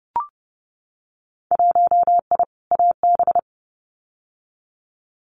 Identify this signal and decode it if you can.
Morse code, heard as a keyed tone, about 30 wpm, 700 Hz. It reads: 1I AB
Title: Winter Field Day Exchanges Encore 30wpm